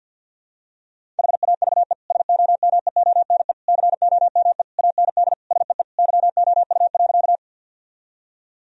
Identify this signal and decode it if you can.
Morse code, heard as a keyed tone, about 50 wpm, 700 Hz. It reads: HAVE SOMEONE COME AND SEE YOU.
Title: Sentences from Top 1000 Words - Review Part 4 - 50wpm